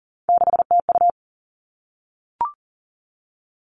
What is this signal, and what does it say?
Morse code, heard as a keyed tone, about 40 wpm, 700 Hz. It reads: 6TU
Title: All Cut Numbers - 3-Digit 40wpm